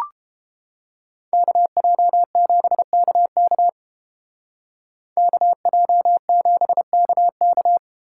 Morse code, heard as a keyed tone, about 32 wpm, 700 Hz. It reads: KJ7KK KJ7KK